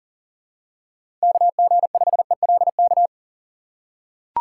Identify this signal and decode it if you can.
Morse code, heard as a keyed tone, about 37 wpm, 700 Hz. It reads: KG5ELK
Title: USA Call Signs - 6 Characters - 2x Character Spacing - 40wpm